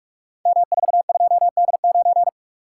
Morse code, heard as a keyed tone, about 44 wpm, 690 Hz. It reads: M42D9
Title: Code Groups - 5 Letter:Numbers - 45wpm